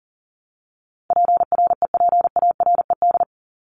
Morse code, heard as a keyed tone, about 40 wpm, 700 Hz. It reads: PREPARED
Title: ED Suffix - Top 100 Words with Base Words 40wpm